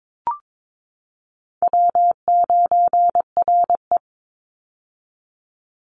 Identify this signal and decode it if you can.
Morse code, heard as a keyed tone, about 22 wpm, 700 Hz. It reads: W9RE
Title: Easy Call Signs - 22wpm